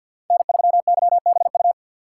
Morse code, heard as a keyed tone, about 52 wpm, 700 Hz. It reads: N3YBU
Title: USA Call Signs - 5 Characters - 2x Character Spacing - 50wpm